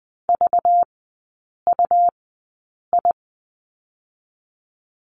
Morse code, heard as a keyed tone, about 20 wpm, 700 Hz. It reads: V U I
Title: 205 Lesson 33e - Letter j - Three Characters 20wpm